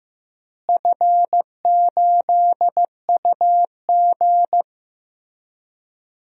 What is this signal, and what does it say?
Morse code, heard as a keyed tone, about 15 wpm, 700 Hz. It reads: F8UG
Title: DXCC-100 Call Signs - 4 Characters - Repeated 2x - 15wpm